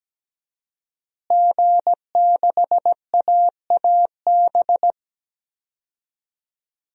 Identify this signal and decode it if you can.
Morse code, heard as a keyed tone, about 17 wpm, 700 Hz. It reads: G6AAB
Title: DXCC-100 Call Signs - 5 Characters - Repeated 2x - 17wpm